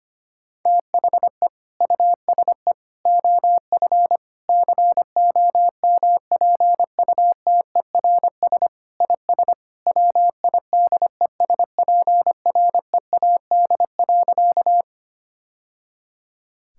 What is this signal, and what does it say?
Morse code, heard as a keyed tone, about 25 wpm, 700 Hz. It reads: THE USE OF COMPUTERS IS WIDESPREAD.
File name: practice / 25wpm / Sentences with 3-Word Phrases - 25wpm